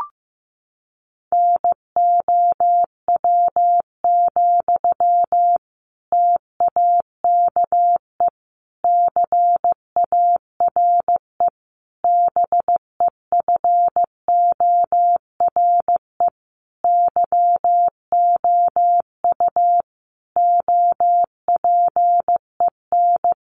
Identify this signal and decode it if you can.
Morse code, heard as a keyed tone, about 15 wpm, 700 Hz. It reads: NOW, TAKE CARE BEFORE YOU OPEN
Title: Sentences from Top 1500 Words 15wpm